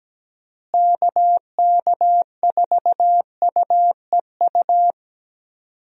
Morse code, heard as a keyed tone, about 17 wpm, 700 Hz. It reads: KK4UEU
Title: USA Call Signs - 6 Characters - Repeated 2x - 17wpm